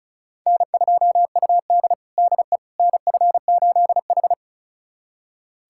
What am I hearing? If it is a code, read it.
Morse code, heard as a keyed tone, about 36 wpm, 700 Hz. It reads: N2UD DE NF8H
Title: USA Callsign DE Callsign - 4 Characters - 35wpm